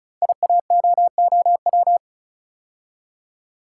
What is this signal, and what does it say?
Morse code, heard as a keyed tone, about 36 wpm, 700 Hz. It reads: IAOOW
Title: Code Groups - 5 Letters - 35wpm